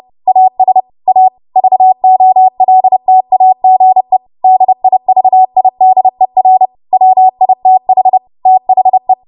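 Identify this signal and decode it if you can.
Morse code, heard as a keyed tone, about 30 wpm, 750 Hz. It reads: AS A VOLTAGE DIVIDER WITH THE
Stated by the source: W1AW Web Code Practice Files, 30 WPM run from ARRL; text from April 2023 QST.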